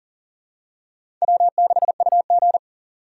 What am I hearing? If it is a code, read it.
Morse code, heard as a keyed tone, about 40 wpm, 700 Hz. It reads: W6UG
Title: USA Call Signs - 4 Characters - 4x Character Spacing - 40wpm